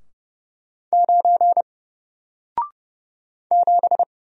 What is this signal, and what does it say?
Morse code, heard as a keyed tone, about 30 wpm, 700 Hz. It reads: 9 7
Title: Single Character (Letter:Number:Punctuation:Prosign) Alt 30wpm